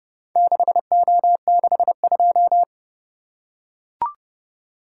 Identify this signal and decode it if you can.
Morse code, heard as a keyed tone, about 30 wpm, 700 Hz. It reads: 6O62